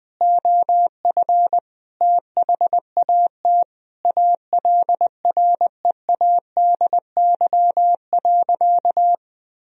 Morse code, heard as a keyed tone, about 20 wpm, 700 Hz. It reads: OF THAT ALREADY.
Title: Sentences from Top 1000 Words - Review Part 3 - 20wpm